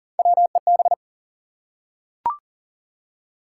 Morse code, heard as a keyed tone, about 41 wpm, 700 Hz. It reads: WEB
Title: Three Letter Words - 40wpm